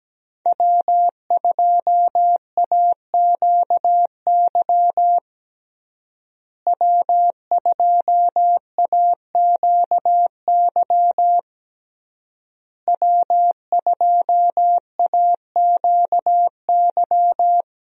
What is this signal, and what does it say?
Morse code, heard as a keyed tone, about 17 wpm, 700 Hz. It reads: W2AQY W2AQY W2AQY
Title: USA Call Signs - 5 Characters - Repeated 3x - 17wpm